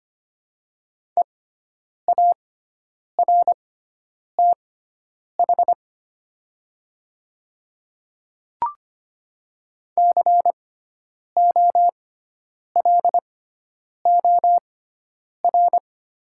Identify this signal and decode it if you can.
Morse code, heard as a keyed tone, about 25 wpm, spaced as for 9 wpm, 700 Hz. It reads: EARTH COLOR
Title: 6x Character Spacing - Top 300 Words - 25wpm